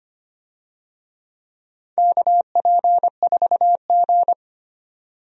Morse code, heard as a keyed tone, about 25 wpm, 700 Hz. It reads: KP4G
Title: DXCC-100 Call Signs - 4 Characters - 25wpm